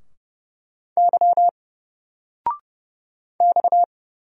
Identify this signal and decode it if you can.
Morse code, heard as a keyed tone, about 31 wpm, 700 Hz. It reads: Y X
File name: Single-Letter-Number-30wpm